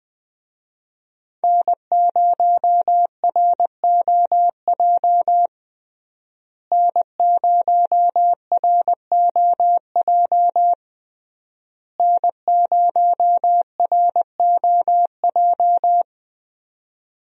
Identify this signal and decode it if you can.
Morse code, heard as a keyed tone, about 21 wpm, 700 Hz. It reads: N0ROJ N0ROJ N0ROJ
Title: USA Call Signs - 5 Characters - Repeated 3x - 20wpm